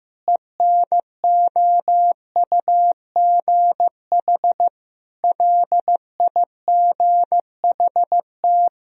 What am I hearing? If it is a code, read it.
Morse code, heard as a keyed tone, about 15 wpm, 700 Hz. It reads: ENOUGH LIGHT
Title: Sentences from Top 1000 Words - Review Part 1 - 15wpm